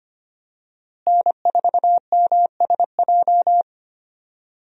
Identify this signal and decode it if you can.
Morse code, heard as a keyed tone, about 25 wpm, 700 Hz. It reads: N4MSJ
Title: USA Call Signs - 5 Characters - Repeated 2x - 25wpm